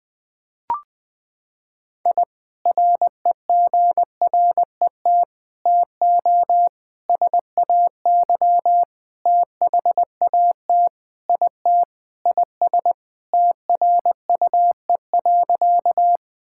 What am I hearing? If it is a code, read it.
Morse code, heard as a keyed tone, about 20 wpm, 700 Hz. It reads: I REGRET TO SAY THAT IT IS TRUE.
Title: Sentences with 3-Word Phrases - 20wpm